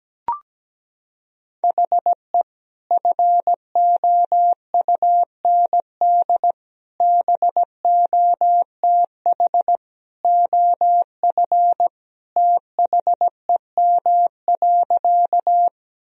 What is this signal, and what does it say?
Morse code, heard as a keyed tone, about 17 wpm, 700 Hz. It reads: HE FOUND BOTH OF THEM.
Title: Sentences from Top 1000 Words - Review Part 1 - 17wpm